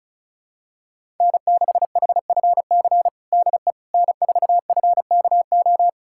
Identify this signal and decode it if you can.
Morse code, heard as a keyed tone, about 34 wpm, 700 Hz. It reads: N6HFC DE N4FKO
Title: USA Callsign DE Callsign - 5 Characters - 35wpm